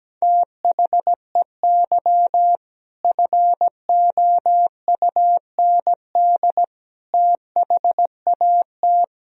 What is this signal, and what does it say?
Morse code, heard as a keyed tone, about 17 wpm, 700 Hz. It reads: THEY FOUND THAT
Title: Sentences from Top 1000 Words - Review Part 4 - 17wpm